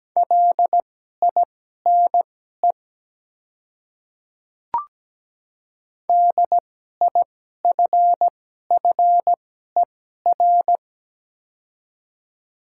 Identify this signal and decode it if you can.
Morse code, heard as a keyed tone, about 17 wpm, spaced as for 12 wpm, 700 Hz. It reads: LINE DIFFER
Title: 2x Character Spacing - Top 200 Words - 17wpm